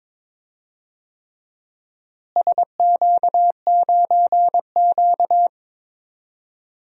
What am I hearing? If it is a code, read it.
Morse code, heard as a keyed tone, about 22 wpm, 700 Hz. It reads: SQ9Q
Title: DXCC-100 Call Signs - 4 Characters - Repeated 3x - 22wpm